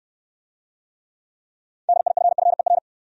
Morse code, heard as a keyed tone, about 65 wpm, 700 Hz. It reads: DIFFER